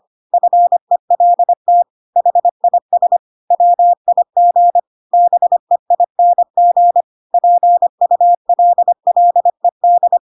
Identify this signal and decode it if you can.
Morse code, heard as a keyed tone, about 25 wpm, 700 Hz. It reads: FELT HIS WIG BEING PULLED
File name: Pinocchio / Pinocchio ch3 25 25 ik1hge